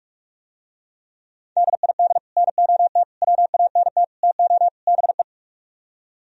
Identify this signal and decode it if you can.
Morse code, heard as a keyed tone, about 50 wpm, 700 Hz. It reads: DID NOT WANT TO BE